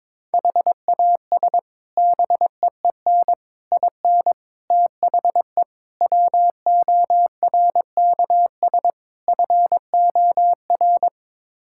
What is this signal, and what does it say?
Morse code, heard as a keyed tone, about 22 wpm, 700 Hz. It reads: HAS BEEN IN THE WORKS FOR
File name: practice / 22wpm / Sets of 6 Words - 22wpm